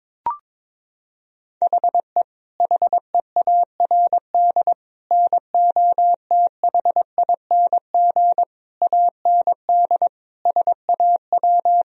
Morse code, heard as a keyed tone, about 22 wpm, 700 Hz. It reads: HE HEARD NOTHING AND SAW
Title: Sentences from Top 1000 Words - Review Part 3 - 22wpm